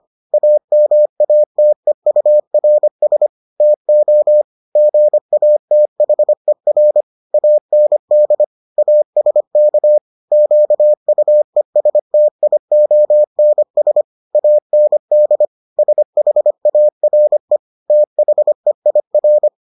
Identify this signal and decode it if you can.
Morse code, heard as a keyed tone, about 25 wpm, 600 Hz. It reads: AMATEURS TO GATHER AND ASK QUESTIONS AND SHARE THEIR